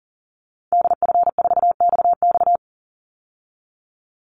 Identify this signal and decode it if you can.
Morse code, heard as a keyed tone, about 40 wpm, 700 Hz. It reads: DF4XX
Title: Call Signs Repeated 3x - 40wpm